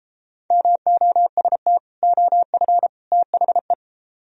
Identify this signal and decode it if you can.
Morse code, heard as a keyed tone, about 33 wpm, 700 Hz. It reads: MOST OF THE